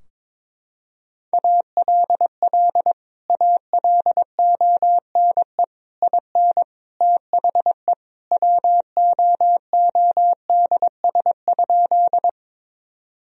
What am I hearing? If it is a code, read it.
Morse code, heard as a keyed tone, about 22 wpm, 700 Hz. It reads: ALL ALONE IN THE WOODS?
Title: Sentences from Top 1000 Words - Review Part 2 - 22wpm